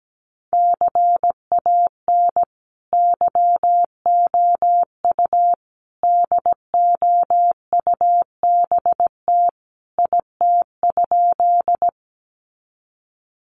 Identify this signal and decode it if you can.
Morse code, heard as a keyed tone, about 17 wpm, 700 Hz. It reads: CAN YOU DOUBT IT?
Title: Sentences from Top 1500 Words 17wpm